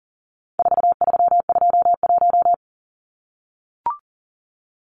Morse code, heard as a keyed tone, about 40 wpm, 700 Hz. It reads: 4321
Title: Numbers - 4-Digit 40wpm